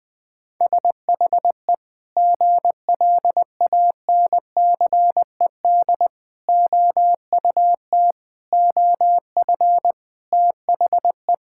Morse code, heard as a keyed tone, about 20 wpm, 700 Hz. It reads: SHE GLANCED OUT OF THE
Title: Sentences with 5-Word Phrases - 20wpm